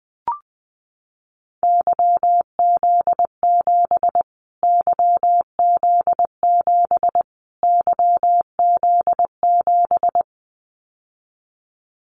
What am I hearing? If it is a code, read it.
Morse code, heard as a keyed tone, about 20 wpm, 700 Hz. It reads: YZ7 YZ7 YZ7
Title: Callsign Prefixes for Top 25 DXCC - Repeated 3x - 20wpm